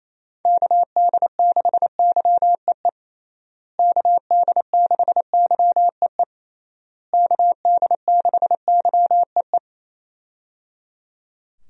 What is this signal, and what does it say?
Morse code, heard as a keyed tone, about 27 wpm, 700 Hz. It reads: KD6YEE KD6YEE KD6YEE
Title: USA Call Signs - 6 Characters - Repeated 3x - 28wpm